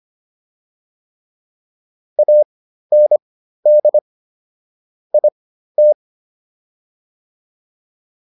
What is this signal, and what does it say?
Morse code, heard as a keyed tone, about 25 wpm, spaced as for 13 wpm, 600 Hz. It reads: AND IT